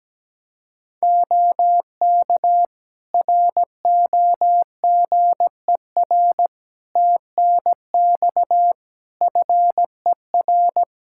Morse code, heard as a keyed tone, about 17 wpm, 700 Hz. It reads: OK ROGER TNX FER
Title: Practice QSO - Exchange 4 - Ending - 17wpm